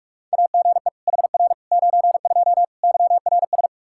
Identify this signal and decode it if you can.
Morse code, heard as a keyed tone, about 46 wpm, 700 Hz. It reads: AGE HR 92 YRS
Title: K1USN SST QSO Practice 45wpm